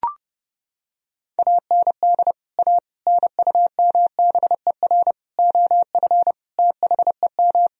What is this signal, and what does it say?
Morse code, heard as a keyed tone, about 32 wpm, 700 Hz. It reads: AND A NUMBER OF THEM
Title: Sentences from Top 1000 Words - Review Part 1 - 30wpm